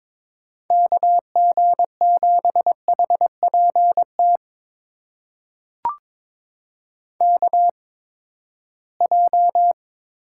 Morse code, heard as a keyed tone, about 22 wpm, spaced as for 15 wpm, 700 Hz. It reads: KG7HPT KJ
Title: USA Call Signs - 6 Characters - 8x Character Spacing - 22wpm